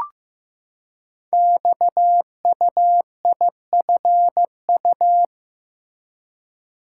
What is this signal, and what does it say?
Morse code, heard as a keyed tone, about 15 wpm, 700 Hz. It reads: XUIFU
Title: Code Groups - 5 Letters - 15wpm